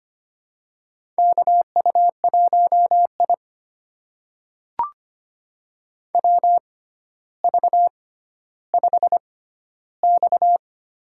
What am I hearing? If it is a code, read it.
Morse code, heard as a keyed tone, about 25 wpm, spaced as for 13 wpm, 700 Hz. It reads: KU1I WV5X
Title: USA Call Signs - 6x Character Spacing - 25wpm